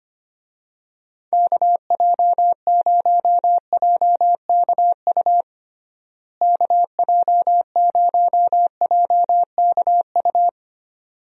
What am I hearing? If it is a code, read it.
Morse code, heard as a keyed tone, about 25 wpm, 700 Hz. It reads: KJ0JKU KJ0JKU
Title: USA Call Signs - 6 Characters - Repeated 2x - 25wpm